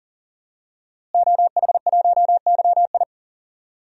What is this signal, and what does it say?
Morse code, heard as a keyed tone, about 40 wpm, 700 Hz. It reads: OH1YI
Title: DXCC-100 Call Signs - 5 Characters - Repeated 2x - 40wpm